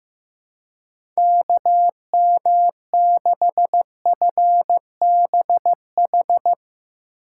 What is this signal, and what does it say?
Morse code, heard as a keyed tone, about 15 wpm, 700 Hz. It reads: KM6FBH
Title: USA Call Signs - 6 Characters - 3x Character Spacing - 15wpm